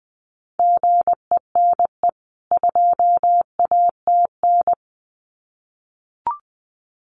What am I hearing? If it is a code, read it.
Morse code, heard as a keyed tone, about 20 wpm, 700 Hz. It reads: GENE 2ATN
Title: CWT Exchange 20wpm